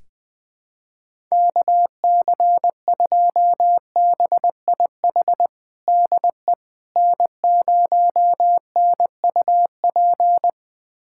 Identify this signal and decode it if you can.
Morse code, heard as a keyed tone, about 21 wpm, 700 Hz. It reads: KC2BIH DE N0NUP
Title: USA Callsign DE Callsign 20wpm